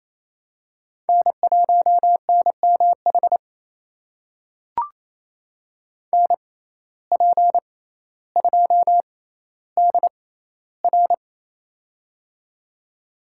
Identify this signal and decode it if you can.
Morse code, heard as a keyed tone, about 27 wpm, spaced as for 14 wpm, 700 Hz. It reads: N1NMH NP2DR
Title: USA Call Signs - 5 Characters - 6x Character Spacing - 28wpm